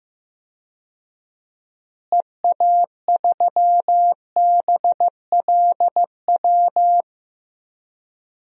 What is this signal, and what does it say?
Morse code, heard as a keyed tone, about 15 wpm, 700 Hz. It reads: EA3BLW
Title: DXCC-100 Call Signs - 6 Characters - Repeated 3x - 15wpm